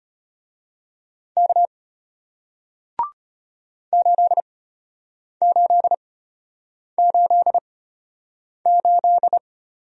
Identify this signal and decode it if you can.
Morse code, heard as a keyed tone, about 34 wpm, spaced as for 8 wpm, 700 Hz. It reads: K 8888